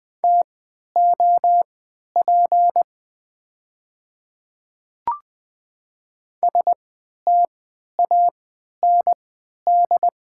Morse code, heard as a keyed tone, about 20 wpm, spaced as for 11 wpm, 700 Hz. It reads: TOP STAND
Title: 3x Character Spacing - Top 500 Words - 20wpm